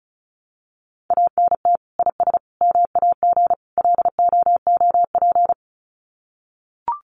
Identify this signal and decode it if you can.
Morse code, heard as a keyed tone, about 34 wpm, 700 Hz. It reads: ANT IS MAG LOOP
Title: Common QSO Phrases 35wpm